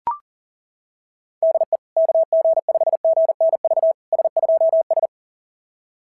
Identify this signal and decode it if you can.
Morse code, heard as a keyed tone, about 38 wpm, 690 Hz. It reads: DE KG5GNV S2S
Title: SOTA QSO Practice 40wpm